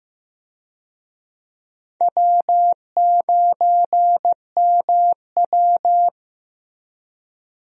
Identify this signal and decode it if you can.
Morse code, heard as a keyed tone, about 15 wpm, 700 Hz. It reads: W9MW
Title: USA Call Signs - 4 Characters - Repeated 2x - 15wpm